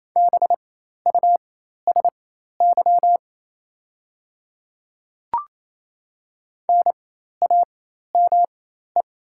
Morse code, heard as a keyed tone, about 28 wpm, spaced as for 13 wpm, 700 Hz. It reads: BUSY NAME
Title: 4x Character Spacing - Top 500 Words - 28wpm